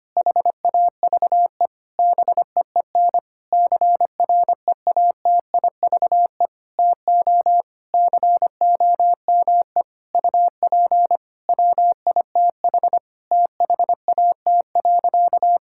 Morse code, heard as a keyed tone, about 25 wpm, 700 Hz. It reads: HAVE BEEN CREATIVE TO COME UP WITH THAT.
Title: Sentences from Top 1000 Words - Review Part 5 - 25wpm